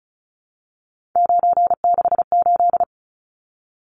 Morse code, heard as a keyed tone, about 34 wpm, 700 Hz. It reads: 968
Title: Numbers - 3-Digit 35wpm